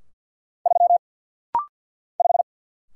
Morse code, heard as a keyed tone, about 48 wpm, 700 Hz. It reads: 3 5
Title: Single Letter-Number - Rapid-Fire 50wpm